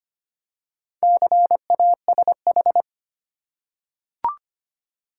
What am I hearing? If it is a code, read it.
Morse code, heard as a keyed tone, about 25 wpm, 700 Hz. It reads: CASH
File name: practice / 25wpm / Easy Words with Hints - 25wpm